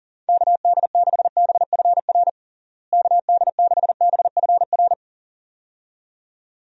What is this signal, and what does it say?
Morse code, heard as a keyed tone, about 40 wpm, 700 Hz. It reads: KD6BFR KD6BFR
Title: USA Call Signs - 6 Characters - Repeated 2x - 40wpm